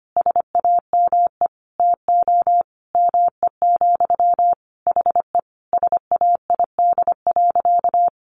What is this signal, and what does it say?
Morse code, heard as a keyed tone, about 25 wpm, 700 Hz. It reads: SAME TO ME, HE SAID.
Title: Sentences from Top 600 Words 25wpm